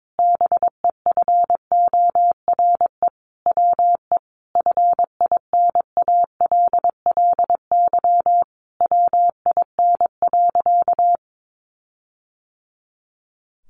Morse code, heard as a keyed tone, about 22 wpm, 700 Hz. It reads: BEFORE WE FINALLY WIN.
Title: Sentences from Top 1500 Words 22wpm